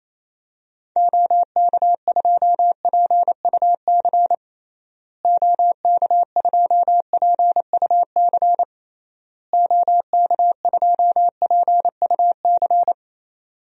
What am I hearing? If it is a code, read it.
Morse code, heard as a keyed tone, about 28 wpm, 700 Hz. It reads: OK2PUC OK2PUC OK2PUC
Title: DXCC-100 Call Signs - 6 Characters - Repeated 3x - 28wpm